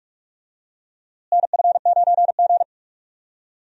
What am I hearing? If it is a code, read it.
Morse code, heard as a keyed tone, about 46 wpm, 700 Hz. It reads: NF9G
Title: USA Call Signs - 4 Characters - Repeated 3x - 45wpm